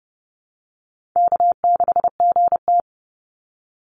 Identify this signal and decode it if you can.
Morse code, heard as a keyed tone, about 30 wpm, 700 Hz. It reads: K6GT
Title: Easy Call Signs Repeated 3x - 30wpm